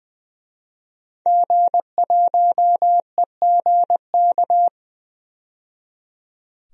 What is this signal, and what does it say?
Morse code, heard as a keyed tone, about 20 wpm, 700 Hz. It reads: G1EGK
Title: DXCC-100 Call Signs - 5 Characters - 20wpm